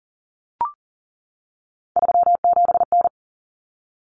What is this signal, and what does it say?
Morse code, heard as a keyed tone, about 40 wpm, 700 Hz. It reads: 37N
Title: Common Cut Numbers - 3-Digit 40wpm